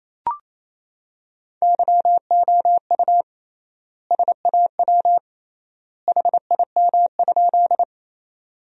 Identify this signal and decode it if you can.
Morse code, heard as a keyed tone, about 28 wpm, 700 Hz. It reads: YOU SAW HIM?
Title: 3x Word-Spacing - Sentences from Top 300 Words - 28wpm